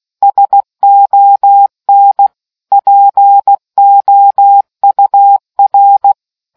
Morse code, heard as a keyed tone, about 16 wpm, 800 Hz. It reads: SON POUR